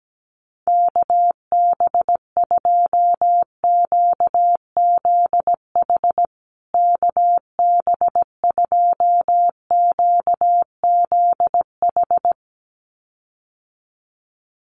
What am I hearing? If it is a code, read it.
Morse code, heard as a keyed tone, about 17 wpm, 700 Hz. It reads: KB2QZH KB2QZH
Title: USA Call Signs Repeated 2x 17wpm